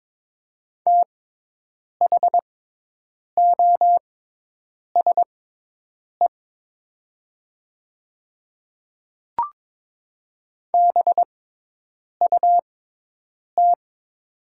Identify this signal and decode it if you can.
Morse code, heard as a keyed tone, about 22 wpm, spaced as for 8 wpm, 690 Hz. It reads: THOSE BUT
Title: 6x Character Spacing - Top 500 Words - 22wpm